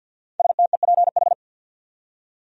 Morse code, heard as a keyed tone, about 46 wpm, 700 Hz. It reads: STEPH
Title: USA Common Names - 50wpm